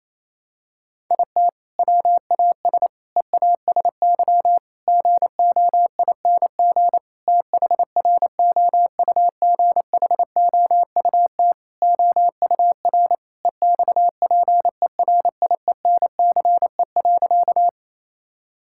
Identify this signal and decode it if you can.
Morse code, heard as a keyed tone, about 27 wpm, 700 Hz. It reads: IT WAS EASY GOING THROUGHOUT OUR EXPERIENCE.
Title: Sentences from Top 1000 Words - Review Part 5 - 28wpm